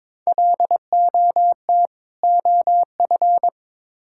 Morse code, heard as a keyed tone, about 22 wpm, 700 Hz. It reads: LOT OF
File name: Build A Phrase - 2-Word Phrases - 22wpm